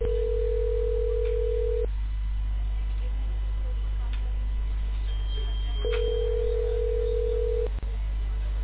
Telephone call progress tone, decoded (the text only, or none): ringback tone